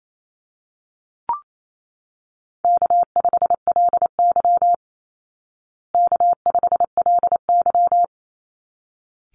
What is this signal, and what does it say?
Morse code, heard as a keyed tone, about 28 wpm, 700 Hz. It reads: K5LY K5LY
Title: Easy Call Signs Repeated 2x - 28wpm